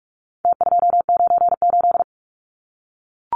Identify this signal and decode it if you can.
Morse code, heard as a keyed tone, about 44 wpm, 700 Hz. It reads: T298
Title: Numbers - Contest Style - 4-Digit 45wpm